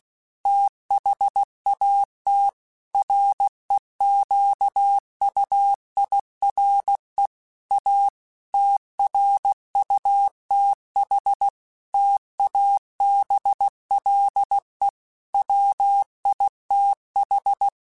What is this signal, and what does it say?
Morse code, heard as a keyed tone, about 16 wpm, 800 Hz. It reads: THAT REQUIRE A TRUTH TABLE WITH
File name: Technical Subjects / Technical Subjects-mp3s / AND Gate @15